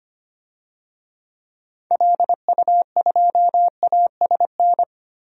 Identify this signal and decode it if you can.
Morse code, heard as a keyed tone, about 25 wpm, 700 Hz. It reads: LU2ASN
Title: DXCC-100 Call Signs - 6 Characters - 25wpm